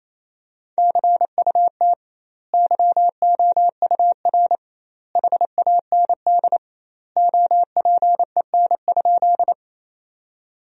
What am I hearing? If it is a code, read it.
Morse code, heard as a keyed tone, about 28 wpm, 700 Hz. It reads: CUT YOUR HAND OPEN?
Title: 2x Word-Spacing - Sentences from Top 300 Words - 28wpm